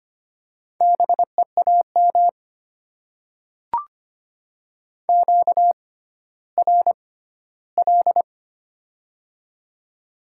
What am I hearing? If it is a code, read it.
Morse code, heard as a keyed tone, about 25 wpm, spaced as for 14 wpm, 690 Hz. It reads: BEAM QRL
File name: Top 90 QSO Elements - 6x Character Spacing - 25wpm